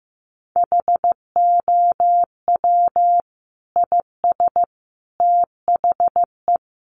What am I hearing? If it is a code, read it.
Morse code, heard as a keyed tone, about 15 wpm, 700 Hz. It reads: HOW IS THE
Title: Sentences from Top 800 Words 15wpm